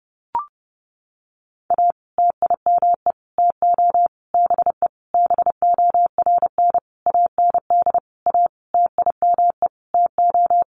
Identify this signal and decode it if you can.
Morse code, heard as a keyed tone, about 30 wpm, 700 Hz. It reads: A TIME TO BE BORN AND A TIME TO
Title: Sentences from Top 1300 Words 30wpm